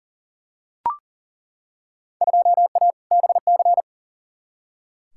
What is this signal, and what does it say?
Morse code, heard as a keyed tone, about 40 wpm, 700 Hz. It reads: 2A BC
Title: Field Day Exchanges - 40wpm